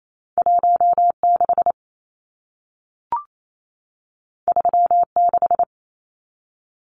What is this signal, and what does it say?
Morse code, heard as a keyed tone, about 28 wpm, 700 Hz. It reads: 16 36
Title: Numbers - 2-Digit 28wpm